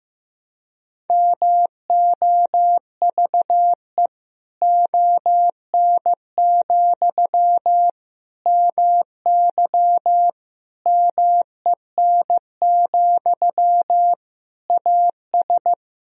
Morse code, heard as a keyed tone, about 15 wpm, 700 Hz. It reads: MOVE ON, MY MEN, AS
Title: Sentences from Top 500 Words and Abbreviations 15wpm